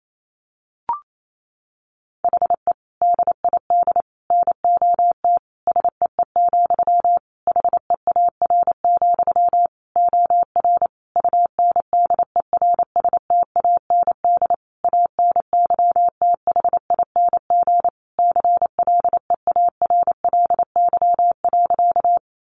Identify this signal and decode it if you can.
Morse code, heard as a keyed tone, about 28 wpm, 700 Hz. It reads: HE DID NOT SEE, HEAR, OR UNDERSTAND ANYTHING CLEARLY.
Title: Sentences from Top 1300 Words 28wpm